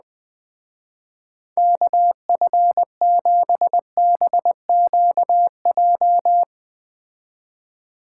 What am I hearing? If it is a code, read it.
Morse code, heard as a keyed tone, about 20 wpm, 700 Hz. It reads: KF7BQJ
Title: Speed Racing - USA 2x3 Call Signs 17wpm